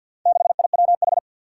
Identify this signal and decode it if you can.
Morse code, heard as a keyed tone, about 48 wpm, 700 Hz. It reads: 6IWH
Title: Four Challenging Letter-Number Combinations 50wpm